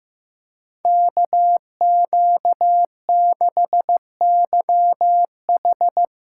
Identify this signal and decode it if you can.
Morse code, heard as a keyed tone, about 16 wpm, 700 Hz. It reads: KQ6YH
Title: USA Callsign DE Callsign 15wpm